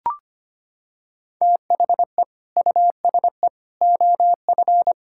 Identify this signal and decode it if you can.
Morse code, heard as a keyed tone, about 25 wpm, 700 Hz. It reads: THE USE OF
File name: practice / 25wpm / Sentences with 2-Word Phrases - 25wpm